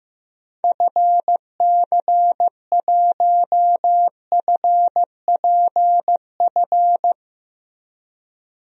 Morse code, heard as a keyed tone, about 15 wpm, 700 Hz. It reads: FC1FPF
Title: DXCC-100 Call Signs - 6 Characters - Repeated 3x - 15wpm